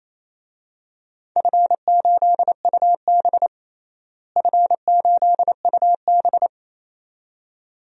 Morse code, heard as a keyed tone, about 28 wpm, 700 Hz. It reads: F8UB F8UB
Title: DXCC-100 Call Signs - 4 Characters - Repeated 2x - 28wpm